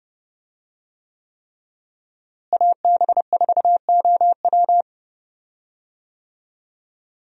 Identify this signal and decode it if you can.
Morse code, heard as a keyed tone, about 32 wpm, 700 Hz. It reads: AB4OW
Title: USA Call Signs - 5 Characters - Repeated 3x - 30wpm